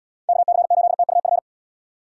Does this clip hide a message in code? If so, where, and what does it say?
Yes: Morse code from 0.3 to 1.4 s, heard as a keyed tone, about 72 wpm, 700 Hz. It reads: KC8TUF